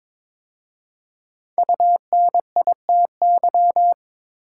Morse code, heard as a keyed tone, about 22 wpm, 700 Hz. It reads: UNITY
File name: 5 Letter Words - First 3 Letters Given - 22wpm